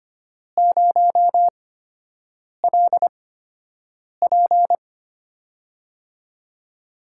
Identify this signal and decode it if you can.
Morse code, heard as a keyed tone, about 25 wpm, spaced as for 7 wpm, 700 Hz. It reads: 0LP
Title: USA Call Signs - 8x Character Spacing - 25wpm